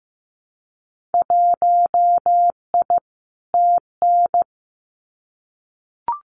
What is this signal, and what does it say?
Morse code, heard as a keyed tone, about 15 wpm, 700 Hz. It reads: 1I TN
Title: Winter Field Day Exchanges 15wpm